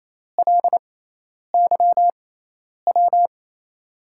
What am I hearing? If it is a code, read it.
Morse code, heard as a keyed tone, about 28 wpm, spaced as for 10 wpm, 700 Hz. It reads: LYW
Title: USA Call Signs - 5 Characters - 6x Character Spacing - 28wpm